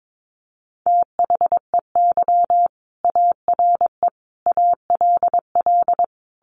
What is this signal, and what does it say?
Morse code, heard as a keyed tone, about 22 wpm, 700 Hz. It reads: THEY ARE ALL
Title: Additional Easy Phrases 22wpm